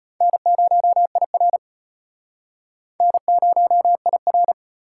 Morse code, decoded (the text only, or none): N0IR N0IR